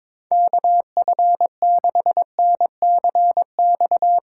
Morse code, heard as a keyed tone, about 21 wpm, 700 Hz. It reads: KF6NCX
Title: K1USN SST QSO Practice - 22wpm